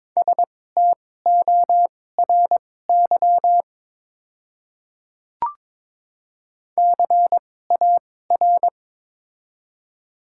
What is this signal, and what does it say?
Morse code, heard as a keyed tone, about 22 wpm, spaced as for 16 wpm, 700 Hz. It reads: STORY CAR
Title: 2x Character Spacing - Top 300 Words - 22wpm